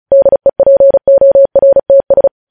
Morse code, heard as a keyed tone, about 36 wpm, 550 Hz. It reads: DEPORTS
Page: News Headlines in Morse Code at 35 WPM